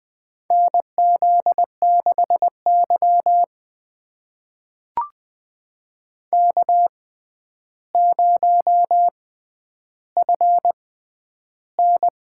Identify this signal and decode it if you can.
Morse code, heard as a keyed tone, about 21 wpm, spaced as for 10 wpm, 700 Hz. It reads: NZ6Y K0FN
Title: USA Call Signs - 6x Character Spacing - 20wpm